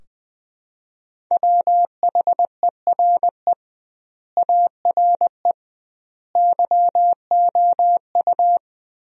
Morse code, heard as a keyed tone, about 20 wpm, 700 Hz. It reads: WHERE ARE YOU
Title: 2x Word-Spacing - Sentences from Top 400 Words - 20wpm